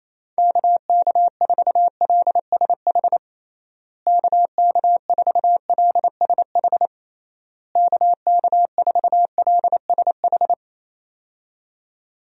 Morse code, heard as a keyed tone, about 28 wpm, 700 Hz. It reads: KK4LSH KK4LSH KK4LSH
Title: USA Call Signs - 6 Characters - Repeated 3x - 28wpm